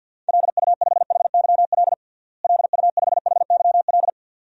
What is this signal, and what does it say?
Morse code, heard as a keyed tone, about 53 wpm, 700 Hz. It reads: LU5HYL LU5HYL
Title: DXCC-100 Call Signs - 6 Characters - Repeated 2x - 50wpm